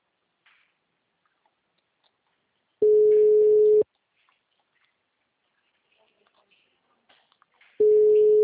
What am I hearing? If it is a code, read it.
Ringback tone (Europe, 425 Hz).